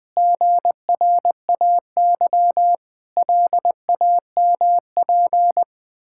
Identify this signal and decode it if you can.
Morse code, heard as a keyed tone, about 21 wpm, 700 Hz. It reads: GRAY LAMP
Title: Pangram Challenge v2 - 20wpm